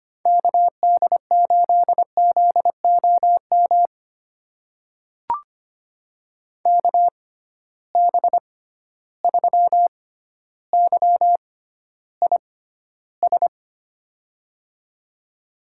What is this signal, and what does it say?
Morse code, heard as a keyed tone, about 25 wpm, spaced as for 13 wpm, 700 Hz. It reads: KD8ZOM KB3YIS
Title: USA Call Signs - 6 Characters - 6x Character Spacing - 25wpm